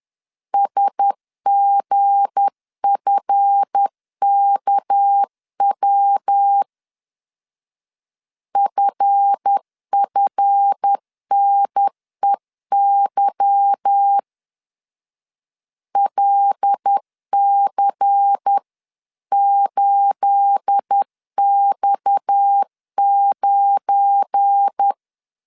Morse code, heard as a keyed tone, about 10 wpm, 800 Hz. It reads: SGFKW FFNEY LC8X9